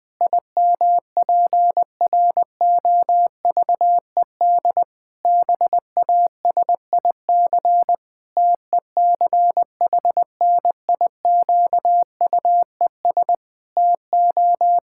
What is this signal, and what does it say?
Morse code, heard as a keyed tone, about 20 wpm, 700 Hz. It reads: IMPROVED BASIC TECHNIQUES TO
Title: Pangram Challenge v2 - 20wpm